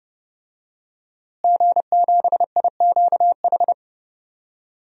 Morse code, heard as a keyed tone, about 30 wpm, 700 Hz. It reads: G7IQH
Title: DXCC-100 Call Signs - 5 Characters - Repeated 2x - 30wpm